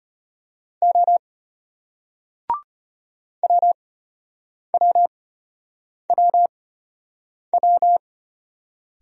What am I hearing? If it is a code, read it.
Morse code, heard as a keyed tone, about 34 wpm, spaced as for 8 wpm, 700 Hz. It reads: O WWWW